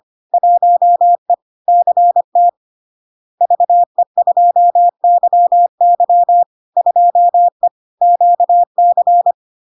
Morse code, heard as a keyed tone, about 25 wpm, 700 Hz. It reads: 1E CT VE2YY 2E QC
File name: LARK-FD-2022-Log-in-CW-25wpm-700-Hz